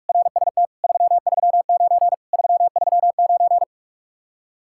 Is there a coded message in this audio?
Yes: Morse code from 0.1 to 3.6 s, heard as a keyed tone, about 44 wpm, 700 Hz. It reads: RST 339 339